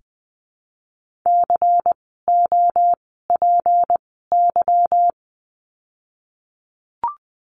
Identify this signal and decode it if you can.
Morse code, heard as a keyed tone, about 20 wpm, spaced as for 15 wpm, 700 Hz. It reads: COPY
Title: 153 Lesson 25f - Letter p - Words 20wpm